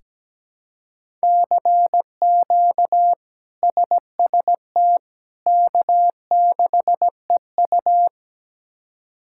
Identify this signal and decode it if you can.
Morse code, heard as a keyed tone, about 17 wpm, 700 Hz. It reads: CQ SST K6EU
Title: K1USN SST QSO Practice - 17wpm